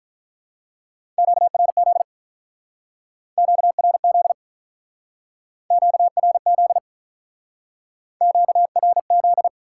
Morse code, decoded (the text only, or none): QRZ QRZ QRZ QRZ